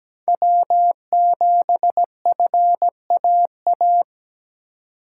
Morse code, heard as a keyed tone, about 17 wpm, 700 Hz. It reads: W7FAA
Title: USA Call Signs - 5 Characters - 3x Character Spacing - 17wpm